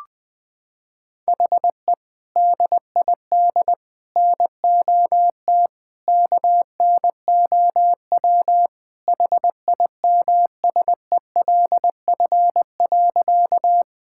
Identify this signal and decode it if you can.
Morse code, heard as a keyed tone, about 20 wpm, 700 Hz. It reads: HE DID NOT KNOW HIMSELF.
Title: Sentences from Top 1600 Words 20wpm